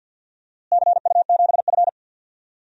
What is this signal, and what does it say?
Morse code, heard as a keyed tone, about 52 wpm, 700 Hz. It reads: CU7F
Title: DXCC-100 Call Signs - 4 Characters - Repeated 2x - 50wpm